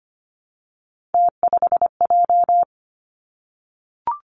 Morse code, heard as a keyed tone, about 25 wpm, 700 Hz. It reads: T5J
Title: Three Challenging Letter-Number Combinations 25wpm